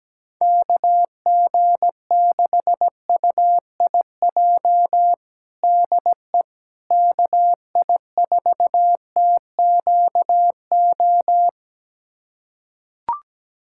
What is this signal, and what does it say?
Morse code, heard as a keyed tone, about 17 wpm, 700 Hz. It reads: KG6UIJ DE KI4TQO
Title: USA Callsign DE Callsign - 6 Characters - 17wpm